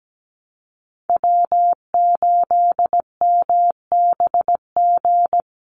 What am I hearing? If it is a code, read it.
Morse code, heard as a keyed tone, about 17 wpm, 700 Hz. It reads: W8MBG
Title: USA Call Signs 17wpm